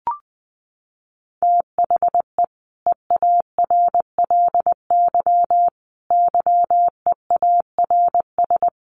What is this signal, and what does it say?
Morse code, heard as a keyed tone, about 20 wpm, 700 Hz. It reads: THE EARLY YEARS
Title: Sets of 5 Words Encore 20wpm